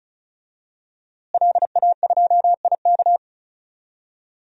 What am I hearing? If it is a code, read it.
Morse code, heard as a keyed tone, about 34 wpm, 700 Hz. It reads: LA2IK
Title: DXCC-100 Call Signs - 5 Characters - Repeated 2x - 35wpm